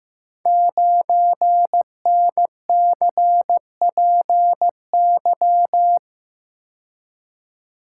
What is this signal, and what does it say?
Morse code, heard as a keyed tone, about 15 wpm, 700 Hz. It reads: 9NCPY
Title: Code Groups - 5 Letter:Numbers - 15wpm